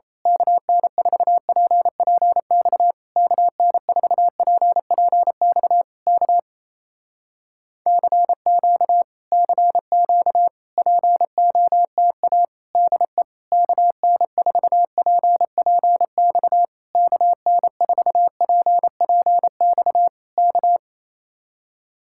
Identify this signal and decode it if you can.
Morse code, heard as a keyed tone, about 27 wpm, 700 Hz. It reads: KN4PPX KN4PPX K CQ CQ POTA DE KN4PPX KN4PPX K
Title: Speed-Racing Common QSO Phrases Encore - 28wpm